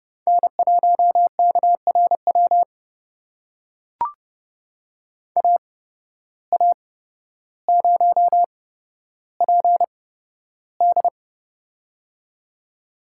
Morse code, heard as a keyed tone, about 30 wpm, spaced as for 13 wpm, 700 Hz. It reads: N1KRW AA0PD
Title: USA Call Signs - 5 Characters - 8x Character Spacing - 30wpm